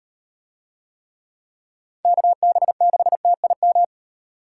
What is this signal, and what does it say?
Morse code, heard as a keyed tone, about 38 wpm, 700 Hz. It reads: KB6TIM